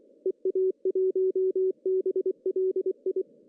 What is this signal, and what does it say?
Morse code, heard as a keyed tone, about 24 wpm, 340 Hz. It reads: EA1BLI
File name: CW / cqww-cw-k5tr